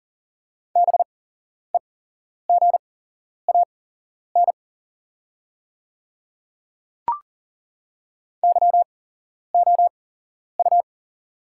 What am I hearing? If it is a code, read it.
Morse code, heard as a keyed tone, about 40 wpm, spaced as for 11 wpm, 690 Hz. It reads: BEGAN YOU
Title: 8x Character Spacing - Top 500 Words - 40wpm